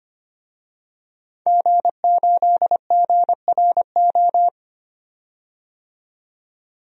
Morse code, heard as a keyed tone, about 25 wpm, 700 Hz. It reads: G8GRO
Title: DXCC-100 Call Signs - 5 Characters - 25wpm